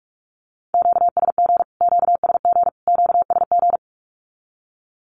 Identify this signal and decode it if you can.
Morse code, heard as a keyed tone, about 44 wpm, 700 Hz. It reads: QSG QSG QSG
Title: Q-Codes Repeated 3x 45wpm